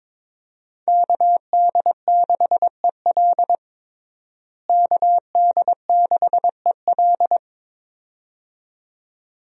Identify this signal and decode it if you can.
Morse code, heard as a keyed tone, about 22 wpm, 700 Hz. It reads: KD6EL KD6EL
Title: USA Call Signs - 5 Characters - Repeated 2x - 22wpm